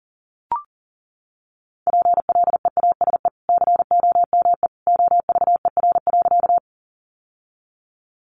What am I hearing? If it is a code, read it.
Morse code, heard as a keyed tone, about 40 wpm, 700 Hz. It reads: PLEASE COME OVER.
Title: Sentences from Top 1500 Words 40wpm